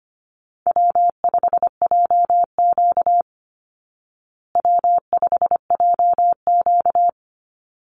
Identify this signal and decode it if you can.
Morse code, heard as a keyed tone, about 25 wpm, 700 Hz. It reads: W5JQ W5JQ
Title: Call Signs Repeated 3x - 25wpm